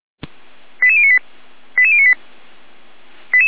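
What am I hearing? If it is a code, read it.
Five-tone selective call (ZVEI), heard as quick runs of five tones, tones of about 70 ms. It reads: 88098 88098
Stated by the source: ZVEI five-tone call received on an SDR.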